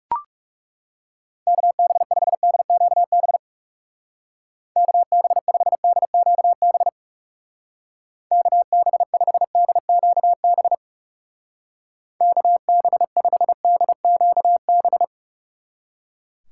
Morse code, heard as a keyed tone, about 35 wpm, 700 Hz. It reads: KB5DQB KB5DQB KB5DQB KB5DQB
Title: Speed Racing - USA Call Signs 30wpm